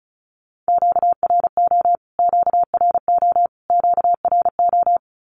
Morse code, heard as a keyed tone, about 34 wpm, 700 Hz. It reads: QRO QRO QRO
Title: Q-Codes Repeated 3x 35wpm